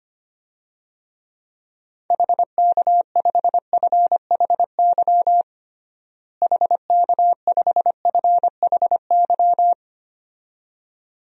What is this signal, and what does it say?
Morse code, heard as a keyed tone, about 25 wpm, 700 Hz. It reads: HK5FHY HK5FHY
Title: DXCC-100 Call Signs - 6 Characters - Repeated 2x - 25wpm